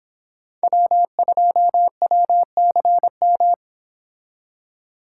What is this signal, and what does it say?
Morse code, heard as a keyed tone, about 26 wpm, 700 Hz. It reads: W2WCM
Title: Speed Racing - USA 1x3 Call Signs 22wpm